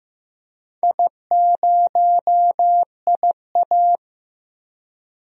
Morse code, heard as a keyed tone, about 16 wpm, 700 Hz. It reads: I0IA
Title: DXCC-100 Call Signs - 4 Characters - 15wpm